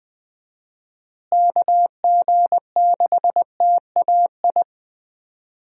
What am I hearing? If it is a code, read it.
Morse code, heard as a keyed tone, about 20 wpm, 700 Hz. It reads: KG6TAI
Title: USA Call Signs - 6 Characters - Repeated 2x - 20wpm